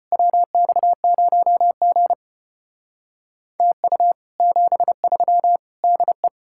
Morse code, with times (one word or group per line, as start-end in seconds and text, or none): 0.1-2.1 WX0G
3.6-4.1 TU
4.4-5.6 73
5.8-6.3 DE